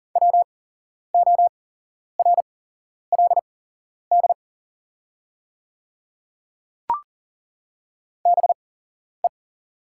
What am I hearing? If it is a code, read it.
Morse code, heard as a keyed tone, about 42 wpm, spaced as for 11 wpm, 700 Hz. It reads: WORLD BE
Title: 8x Character Spacing - Top 200 Words - 40wpm